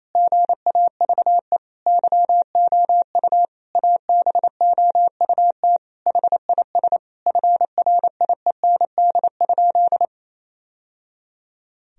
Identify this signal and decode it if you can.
Morse code, heard as a keyed tone, about 28 wpm, 700 Hz. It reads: GAVE YOU ABOUT HIS FRIEND?
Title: Sentences from Top 1000 Words - Review Part 3 - 28wpm